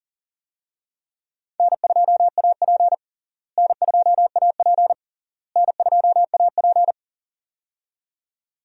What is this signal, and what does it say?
Morse code, heard as a keyed tone, about 37 wpm, 700 Hz. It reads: N2AP N2AP N2AP
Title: USA Call Signs - 4 Characters - Repeated 3x - 40wpm